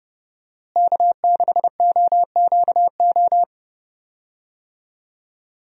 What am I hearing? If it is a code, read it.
Morse code, heard as a keyed tone, about 30 wpm, 700 Hz. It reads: K6OQO
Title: USA Call Signs - 5 Characters - Repeated 3x - 30wpm